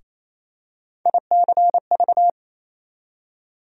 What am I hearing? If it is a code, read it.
Morse code, heard as a keyed tone, about 28 wpm, 700 Hz. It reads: ICV
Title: USA Call Sign Postfixes - 3 Letters 28wpm